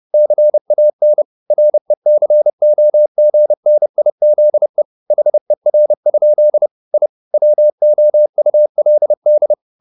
Morse code, heard as a keyed tone, about 30 wpm, 600 Hz. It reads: CAN RECOGNIZE HER? I WOULD